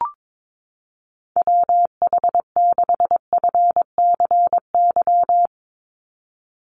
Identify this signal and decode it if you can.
Morse code, heard as a keyed tone, about 22 wpm, 700 Hz. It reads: WH6FCY
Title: USA Call Signs 22wpm